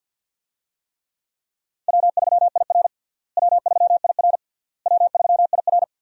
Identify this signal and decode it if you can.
Morse code, heard as a keyed tone, about 51 wpm, 700 Hz. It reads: W3IR W3IR W3IR
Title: USA Call Signs - 4 Characters - Repeated 3x - 50wpm